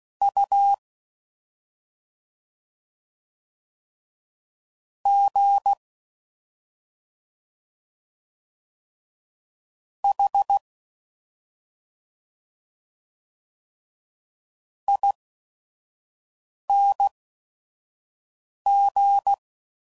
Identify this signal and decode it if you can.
Morse code, heard as a keyed tone, about 16 wpm, spaced as for 5 wpm, 800 Hz. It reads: U G H ING